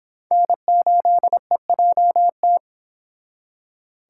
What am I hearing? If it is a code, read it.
Morse code, heard as a keyed tone, about 32 wpm, 700 Hz. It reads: N8EJT